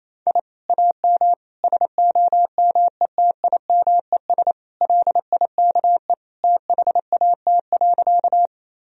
Morse code, heard as a keyed tone, about 28 wpm, 700 Hz. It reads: I AM SOMETIMES LIKE THAT.
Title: Sentences from Top 1000 Words - Review Part 3 - 28wpm